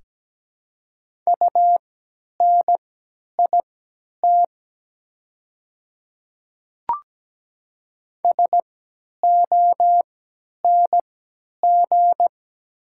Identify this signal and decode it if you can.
Morse code, heard as a keyed tone, about 17 wpm, spaced as for 10 wpm, 690 Hz. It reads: UNIT SONG
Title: 3x Character Spacing - Top 500 Words - 17wpm